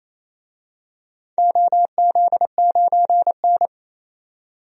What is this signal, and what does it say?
Morse code, heard as a keyed tone, about 28 wpm, 700 Hz. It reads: OZ9N
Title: DXCC-100 Call Signs - 4 Characters - 28wpm